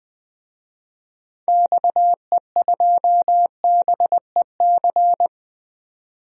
Morse code, heard as a keyed tone, about 21 wpm, 700 Hz. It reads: XE2BEC
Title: DXCC-100 Call Signs - 6 Characters - 20wpm